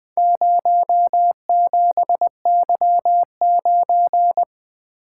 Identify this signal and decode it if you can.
Morse code, heard as a keyed tone, about 20 wpm, 700 Hz. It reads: 07Y9
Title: Four Letter-Number Combinations - 20wpm